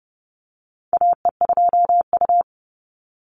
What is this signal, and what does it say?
Morse code, heard as a keyed tone, about 30 wpm, 700 Hz. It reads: AE2U
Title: Call Signs - 30wpm